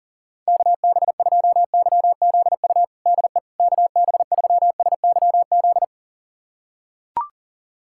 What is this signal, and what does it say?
Morse code, heard as a keyed tone, about 40 wpm, 700 Hz. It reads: KB2YZU DE KB3SYZ
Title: USA Callsign DE Callsign 40wpm